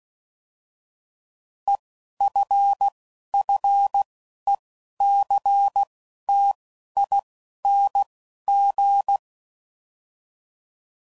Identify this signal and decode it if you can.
Morse code, heard as a keyed tone, about 16 wpm, 800 Hz. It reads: E F F E C T I N G